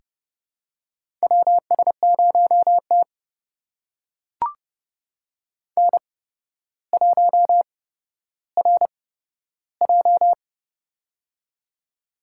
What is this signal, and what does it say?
Morse code, heard as a keyed tone, about 32 wpm, spaced as for 13 wpm, 700 Hz. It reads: WS0T N1RJ
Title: USA Call Signs - 4 Characters - 8x Character Spacing - 30wpm